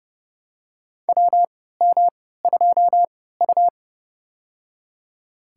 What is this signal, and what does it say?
Morse code, heard as a keyed tone, about 31 wpm, spaced as for 17 wpm, 690 Hz. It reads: WM2U
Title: USA Call Signs - 3x Character Spacing - 30wpm